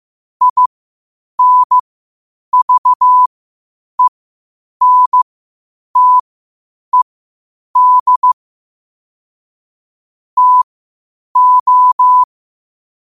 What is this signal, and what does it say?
Morse code, heard as a keyed tone, about 15 wpm, spaced as for 8 wpm, 1000 Hz. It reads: INVENTED TO
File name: qotdfiles / 8wpm